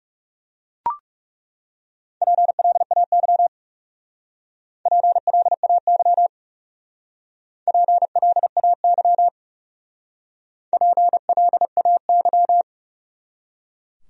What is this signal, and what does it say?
Morse code, heard as a keyed tone, about 41 wpm, 700 Hz. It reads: PLAY PLAY PLAY PLAY